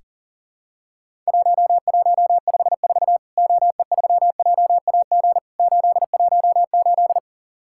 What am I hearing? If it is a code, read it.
Morse code, heard as a keyed tone, about 40 wpm, 700 Hz. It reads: 1154 OE3JAG 818
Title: Worked All Europe DX Contest - QTC Exchange - 40wpm